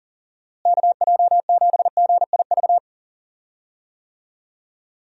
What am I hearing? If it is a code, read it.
Morse code, heard as a keyed tone, about 40 wpm, 700 Hz. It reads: KJ7GIV
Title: USA Call Signs - 6 Characters - Repeated 2x - 40wpm